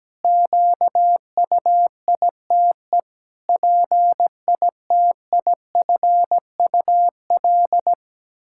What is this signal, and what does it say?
Morse code, heard as a keyed tone, about 17 wpm, 700 Hz. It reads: QUITE PITIFUL
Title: Pangram Challenge v2 - 17wpm